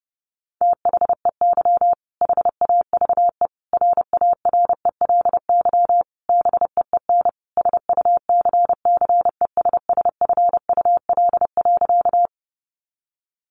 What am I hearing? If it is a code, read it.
Morse code, heard as a keyed tone, about 30 wpm, 700 Hz. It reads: THEY HAVE RARELY BEEN SUCCESSFUL.
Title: Sentences from Top 1400 Words 30wpm